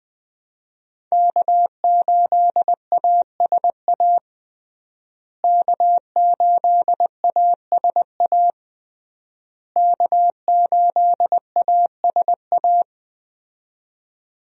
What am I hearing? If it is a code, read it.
Morse code, heard as a keyed tone, about 20 wpm, 700 Hz. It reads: K8ASA K8ASA K8ASA
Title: USA Call Signs - 5 Characters - Repeated 3x - 20wpm